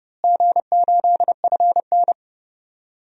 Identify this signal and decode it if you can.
Morse code, heard as a keyed tone, about 30 wpm, 700 Hz. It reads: G8FN
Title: DXCC-100 Call Signs - 4 Characters - Repeated 3x - 30wpm